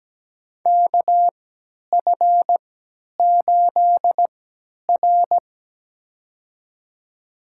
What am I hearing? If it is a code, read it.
Morse code, heard as a keyed tone, about 17 wpm, spaced as for 10 wpm, 700 Hz. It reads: KF8R
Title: USA Call Signs - 4 Characters - 3x Character Spacing - 17wpm